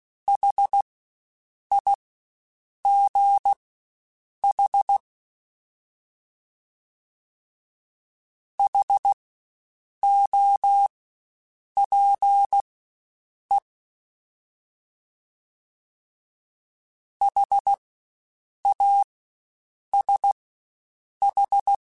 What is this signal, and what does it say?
Morse code, heard as a keyed tone, about 16 wpm, spaced as for 7 wpm, 800 Hz. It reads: HIGH HOPE HASH
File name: Spaced Alphas / Spaced Alphas mp3s / Hs sp @15